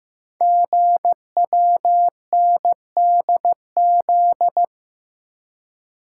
Morse code, heard as a keyed tone, about 15 wpm, 690 Hz. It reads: GWNDZ
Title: Code Groups - 5 Letters - 15wpm